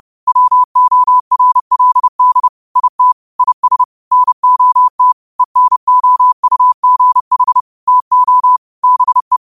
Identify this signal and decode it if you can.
Morse code, heard as a keyed tone, about 30 wpm, 1000 Hz. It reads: WORLD IT IS NOT ENOUGH TO BE